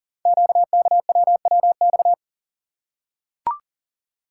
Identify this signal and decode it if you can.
Morse code, heard as a keyed tone, about 40 wpm, 700 Hz. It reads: QKWWX
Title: Code Groups - 5 Letters - 40wpm